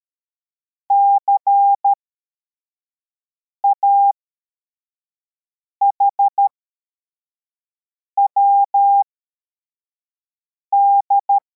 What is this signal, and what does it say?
Morse code, heard as a keyed tone, about 13 wpm, spaced as for 4 wpm, 800 Hz. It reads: CAHWD